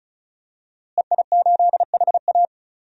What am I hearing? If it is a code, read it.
Morse code, heard as a keyed tone, about 34 wpm, 700 Hz. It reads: EI8HA
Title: DXCC-100 Call Signs - 5 Characters - Repeated 3x - 35wpm